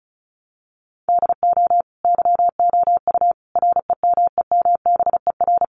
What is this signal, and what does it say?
Morse code, heard as a keyed tone, about 34 wpm, 700 Hz. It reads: DO YOU REMEMBER
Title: Sentences from Top 700 Words 35wpm